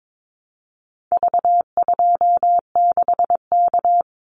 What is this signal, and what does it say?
Morse code, heard as a keyed tone, about 22 wpm, 700 Hz. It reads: V26K
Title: Call Signs Repeated 2x - 22wpm